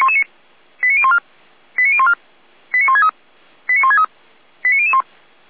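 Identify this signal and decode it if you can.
Five-tone selective call (ZVEI), heard as quick runs of five tones, tones of about 70 ms. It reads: 89014 89014 89163 89163 89001
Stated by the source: ZVEI five-tone selective call 89130 received on SDR.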